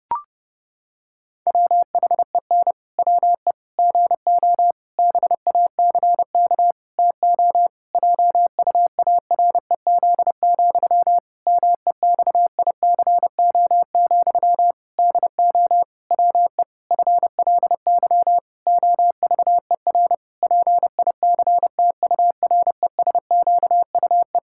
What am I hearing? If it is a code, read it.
Morse code, heard as a keyed tone, about 32 wpm, 700 Hz. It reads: WHEN WE GO BACK TO JUAREZ, MEXICO, DO WE FLY OVER PICTURESQUE
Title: Pangram Challenge v2 - 30wpm